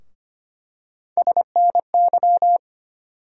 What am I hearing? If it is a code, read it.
Morse code, heard as a keyed tone, about 25 wpm, 700 Hz. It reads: SNY
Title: USA Call Sign Postfixes - 3 Letters 25wpm